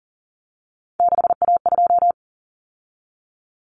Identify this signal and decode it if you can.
Morse code, heard as a keyed tone, about 40 wpm, 700 Hz. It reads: 6A2
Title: Common Cut Numbers - 3-Digit 40wpm